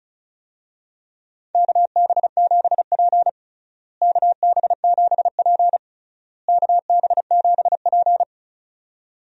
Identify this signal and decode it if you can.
Morse code, heard as a keyed tone, about 34 wpm, 700 Hz. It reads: KB7P KB7P KB7P
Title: USA Call Signs - 4 Characters - Repeated 3x - 35wpm